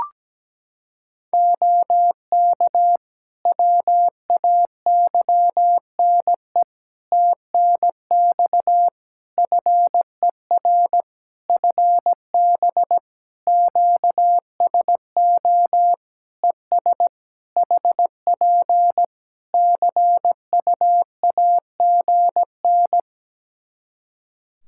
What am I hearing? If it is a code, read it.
Morse code, heard as a keyed tone, about 17 wpm, 700 Hz. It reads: OK WAYNE TNX FER FB QSO ES HP CUAGN
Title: Practice QSO - Exchange 4 - Ending - 17wpm